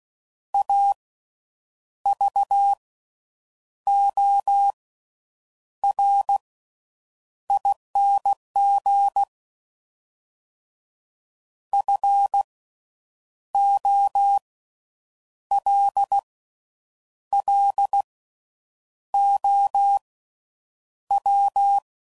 Morse code, heard as a keyed tone, about 16 wpm, spaced as for 7 wpm, 800 Hz. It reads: AVORING FOLLOW
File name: Suffixes / Special INGs / Slow With Faster Ings